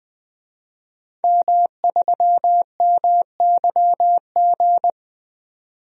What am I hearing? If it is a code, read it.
Morse code, heard as a keyed tone, about 20 wpm, 700 Hz. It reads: M3MYG
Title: DXCC-100 Call Signs - 5 Characters - 20wpm